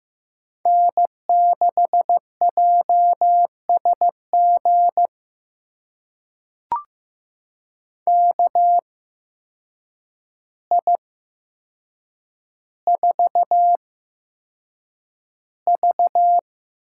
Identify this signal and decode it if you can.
Morse code, heard as a keyed tone, about 15 wpm, spaced as for 7 wpm, 700 Hz. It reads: N6JSG KI4V
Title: USA Call Signs - 8x Character Spacing - 15wpm